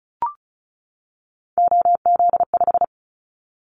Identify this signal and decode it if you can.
Morse code, heard as a keyed tone, about 36 wpm, 700 Hz. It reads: OZ5
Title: Callsign Prefixes for Top 100 DXCC - 35wpm